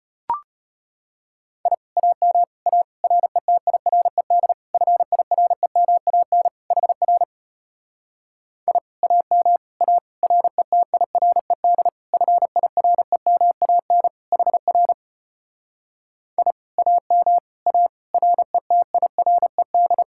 Morse code, heard as a keyed tone, about 33 wpm, 700 Hz. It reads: I AM A RETIRED FIREMAN HR I AM A RETIRED FIREMAN HR I AM A RETIRED